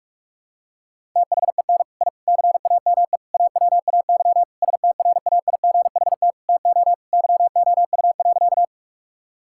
Morse code, heard as a keyed tone, about 42 wpm, 700 Hz. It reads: THEN I CAME AWAY STRAIGHT TO YOU.
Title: Sentences from Top 1000 Words - Review Part 6 - 45wpm